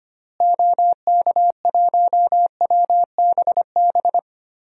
Morse code, heard as a keyed tone, about 25 wpm, 700 Hz. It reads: OK1WBB
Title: DXCC-100 Call Signs - 6 Characters - Repeated 2x - 25wpm